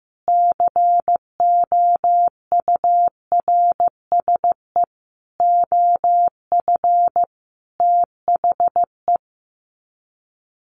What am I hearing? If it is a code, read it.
Morse code, heard as a keyed tone, about 15 wpm, 700 Hz. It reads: COURSE OF THE
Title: Sets of 5 Words Encore 15wpm